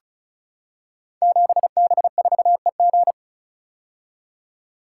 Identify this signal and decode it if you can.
Morse code, heard as a keyed tone, about 36 wpm, 700 Hz. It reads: 7B4EG
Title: Code Groups - 5 Letter:Numbers - 35wpm